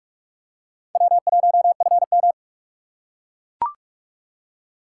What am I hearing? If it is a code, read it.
Morse code, heard as a keyed tone, about 44 wpm, 700 Hz. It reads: W1FM
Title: USA Call Signs - 6x Character Spacing - 45wpm